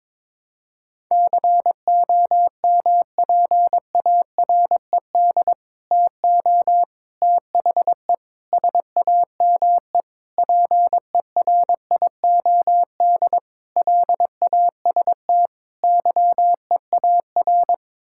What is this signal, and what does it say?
Morse code, heard as a keyed tone, about 22 wpm, 700 Hz. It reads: COMPARED TO THE SAME PERIOD LAST YEAR
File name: practice / 22wpm / Sets of 7 Words - 22wpm